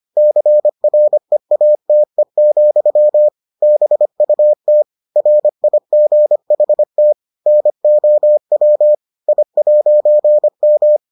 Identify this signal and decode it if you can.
Morse code, heard as a keyed tone, about 25 wpm, 600 Hz. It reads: CREATE, BUT RIGHT NOW I'M